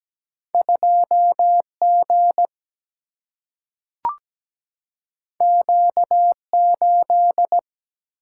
Morse code, heard as a keyed tone, about 17 wpm, 700 Hz. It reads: 2G Q8
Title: Single-Letter Single-Number 17wpm